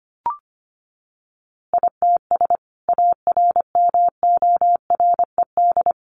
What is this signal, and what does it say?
Morse code, heard as a keyed tone, about 25 wpm, 700 Hz. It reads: ITS ARMORED
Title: A Sound Of Thunder 25wpm